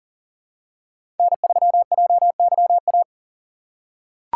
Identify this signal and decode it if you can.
Morse code, heard as a keyed tone, about 39 wpm, 700 Hz. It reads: N3JYA
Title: USA Call Signs - 5 Characters - 3x Character Spacing - 40wpm